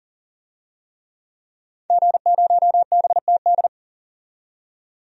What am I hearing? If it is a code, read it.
Morse code, heard as a keyed tone, about 40 wpm, 700 Hz. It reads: G0BTD
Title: DXCC-100 Call Signs - 5 Characters - 40wpm